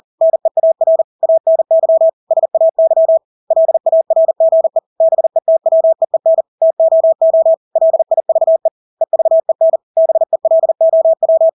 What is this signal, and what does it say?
Morse code, heard as a keyed tone, about 40 wpm, 650 Hz. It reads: NEAR ANY SAY LARGE BETWEEN TOO LIVE EVEN BELOW
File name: Engelske ord 40-40 wpm